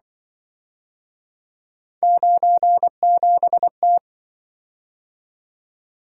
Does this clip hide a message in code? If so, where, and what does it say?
Yes: Morse code from 2.0 to 4.0 s, heard as a keyed tone, about 26 wpm, 700 Hz. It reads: 97T